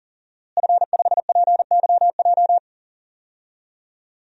Morse code, heard as a keyed tone, about 40 wpm, 700 Hz. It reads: F5PYJ
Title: DXCC-100 Call Signs - 5 Characters - Repeated 2x - 40wpm